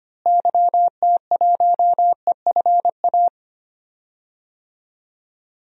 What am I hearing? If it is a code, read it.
Morse code, heard as a keyed tone, about 25 wpm, 700 Hz. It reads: YT1EFA
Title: DXCC-100 Call Signs - 6 Characters - Repeated 2x - 25wpm